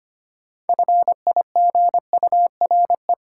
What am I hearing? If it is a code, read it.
Morse code, heard as a keyed tone, about 25 wpm, 700 Hz. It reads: FIGURE
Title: Sentences from Top 1000 Words - Review Part 2 - New Word Drill - 25wpm